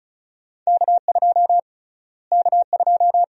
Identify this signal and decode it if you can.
Morse code, heard as a keyed tone, about 34 wpm, 700 Hz. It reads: K2 K2
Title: USA Call Sign Prefixes - Repeated 2x - 35wpm